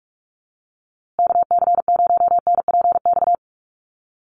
Morse code, heard as a keyed tone, about 44 wpm, 690 Hz. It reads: KC0NPX
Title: USA Call Signs Repeated 2x 45wpm